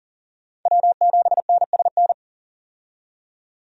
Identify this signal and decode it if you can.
Morse code, heard as a keyed tone, about 40 wpm, 700 Hz. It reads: W7NSN
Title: USA Call Signs - 5 Characters - 40wpm